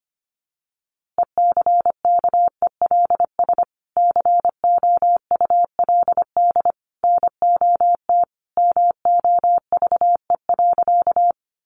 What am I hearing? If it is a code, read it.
Morse code, heard as a keyed tone, about 25 wpm, 700 Hz. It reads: ECKELS COULD NOT MOVE.
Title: A Sound Of Thunder 25wpm